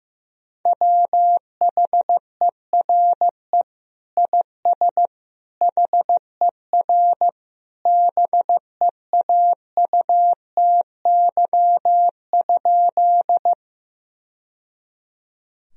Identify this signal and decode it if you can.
Morse code, heard as a keyed tone, about 15 wpm, 700 Hz. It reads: WHERE IS HER BEAUTY?
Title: Sentences from Top 1000 Words - Review Part 2 - 15wpm